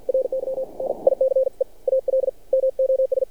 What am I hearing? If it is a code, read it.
Morse code, heard as a keyed tone, about 50 wpm, spaced as for 41 wpm, 530 Hz. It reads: RY NICE ALMOS
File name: archived / CRSnet